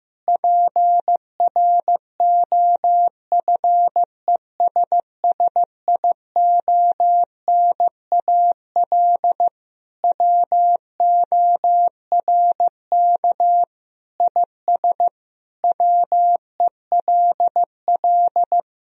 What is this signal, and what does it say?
Morse code, heard as a keyed tone, about 16 wpm, 700 Hz. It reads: PROFESSIONAL WORK IS WELL
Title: Sentences from Top 1000 Words - Review Part 4 - 15wpm